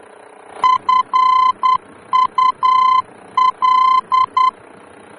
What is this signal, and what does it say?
Morse code, heard as a keyed tone, about 10 wpm, 1020 Hz. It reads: FUL
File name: germany / mx / FUL-D-VOR